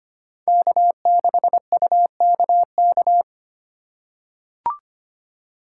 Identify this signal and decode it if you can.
Morse code, heard as a keyed tone, about 25 wpm, 700 Hz. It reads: K6UKK
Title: USA Call Signs - 5 Characters - 3x Character Spacing - 25wpm